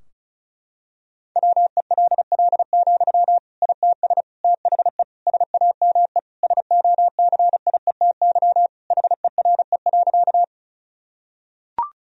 Morse code, heard as a keyed tone, about 34 wpm, 700 Hz. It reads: WELL, ITS THE SAME SOCIETY HERE.
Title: Sentences from Top 1000 Words - Review Part 4 - 35wpm